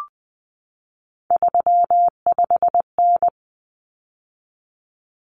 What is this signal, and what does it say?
Morse code, heard as a keyed tone, about 20 wpm, 700 Hz. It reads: 35N
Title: Common Cut Numbers - 3-Digit 20wpm